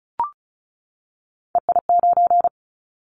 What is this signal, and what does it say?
Morse code, heard as a keyed tone, about 36 wpm, 700 Hz. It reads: EI9
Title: Callsign Prefixes for Top 50 DXCC - 35wpm